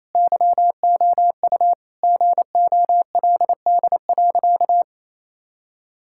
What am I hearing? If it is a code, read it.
Morse code, heard as a keyed tone, about 27 wpm, 700 Hz. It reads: YOU GOLD.
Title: Sentences from Top 1000 Words - Review Part 6 - 28wpm